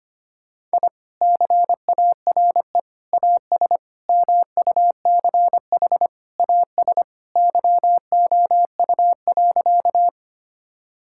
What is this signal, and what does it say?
Morse code, heard as a keyed tone, about 25 wpm, 700 Hz. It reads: I CARE AS MUCH AS YOU.
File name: practice / 25wpm / Sentences with 3-Word Phrases - 25wpm